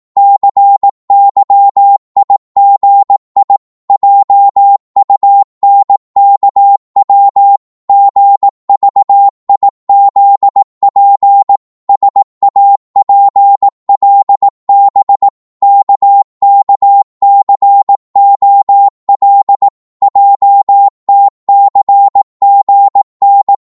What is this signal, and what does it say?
Morse code, heard as a keyed tone, about 18 wpm, 800 Hz. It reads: CYIGI JUNKW GVIZP SAPLB KKCOL JTCGN
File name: Morse / ltr-18wpm